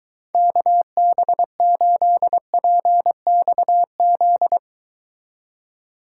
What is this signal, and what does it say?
Morse code, heard as a keyed tone, about 22 wpm, 700 Hz. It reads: KB8PXZ